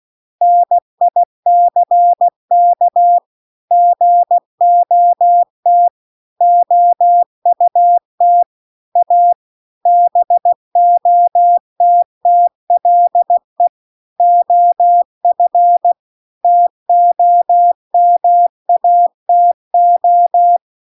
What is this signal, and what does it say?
Morse code, heard as a keyed tone, about 16 wpm, 700 Hz. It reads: NICK GOT OUT A BOTTLE OF TOMATO